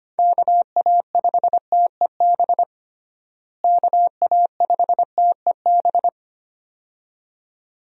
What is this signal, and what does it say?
Morse code, heard as a keyed tone, about 25 wpm, 700 Hz. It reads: KA5TEB KA5TEB
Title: USA Call Signs - 6 Characters - Repeated 2x - 25wpm